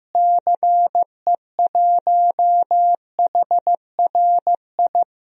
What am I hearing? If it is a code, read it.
Morse code, heard as a keyed tone, about 15 wpm, 700 Hz. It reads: CE1HRI
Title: DXCC-100 Call Signs - 6 Characters - Repeated 2x - 15wpm